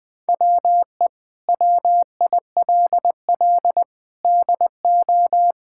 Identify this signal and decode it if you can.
Morse code, heard as a keyed tone, about 21 wpm, 700 Hz. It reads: WE WILL DO
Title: Sentences with 8-Word Phrases - Spoken Beginning - 20wpm